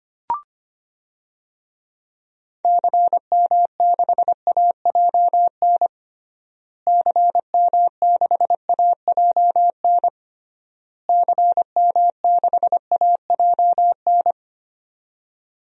Morse code, heard as a keyed tone, about 25 wpm, 700 Hz. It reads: CM6AJN CM6AJN CM6AJN
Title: DXCC-100 Call Signs - 6 Characters - Repeated 3x - 25wpm